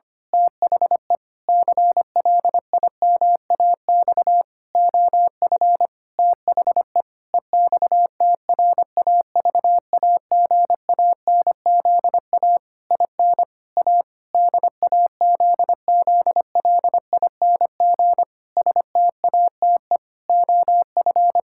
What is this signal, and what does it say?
Morse code, heard as a keyed tone, about 25 wpm, 700 Hz. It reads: THE CLIMAX OF THE EXTRAVAGANZA IN A DAZZLING STATE OF
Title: Pangram Challenge v2 - 25wpm